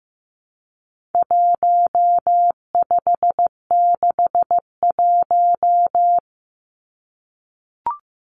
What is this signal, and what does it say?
Morse code, heard as a keyed tone, about 15 wpm, 700 Hz. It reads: 1561
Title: Numbers - 4-Digit 15wpm